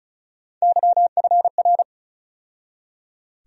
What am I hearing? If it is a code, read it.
Morse code, heard as a keyed tone, about 34 wpm, 700 Hz. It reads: YFR
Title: USA Call Sign Postfixes - 3 Letters 35wpm